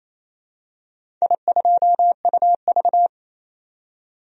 Morse code, heard as a keyed tone, about 27 wpm, 700 Hz. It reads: I2UV
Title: DXCC-100 Call Signs - 4 Characters - 28wpm